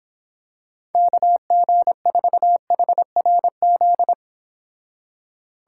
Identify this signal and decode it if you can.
Morse code, heard as a keyed tone, about 26 wpm, 700 Hz. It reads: KG4HRZ